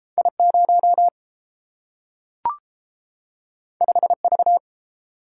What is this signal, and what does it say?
Morse code, heard as a keyed tone, about 30 wpm, 700 Hz. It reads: I0 5V